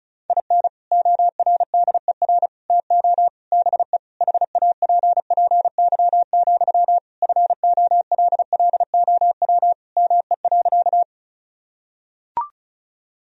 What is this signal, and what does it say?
Morse code, heard as a keyed tone, about 36 wpm, 700 Hz. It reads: IN ORDER TO BE HAPPY, FOLLOW ME.
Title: Sentences from Top 1000 Words - Review Part 2 - 35wpm